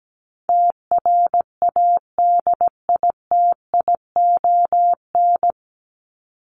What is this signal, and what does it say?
Morse code, heard as a keyed tone, about 17 wpm, 700 Hz. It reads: TRADITION
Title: Top 1251-1500 Words 17wpm